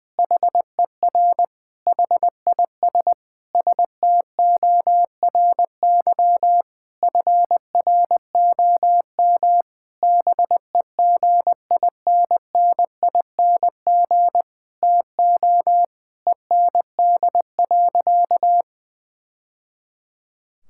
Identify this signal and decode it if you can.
Morse code, heard as a keyed tone, about 20 wpm, 700 Hz. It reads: HER HIS STORY FROM BEGINNING TO END.
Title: Sentences from Top 1000 Words - Review Part 5 - 20wpm